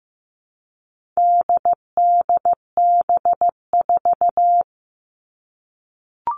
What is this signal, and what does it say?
Morse code, heard as a keyed tone, about 15 wpm, 700 Hz. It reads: DDB4
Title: All Cut Numbers - 4-Digit 15wpm